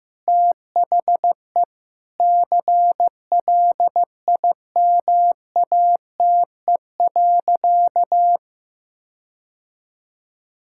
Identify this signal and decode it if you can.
Morse code, heard as a keyed tone, about 15 wpm, 700 Hz. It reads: THE CLIMATE.
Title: Sentences from Top 1600 Words 15wpm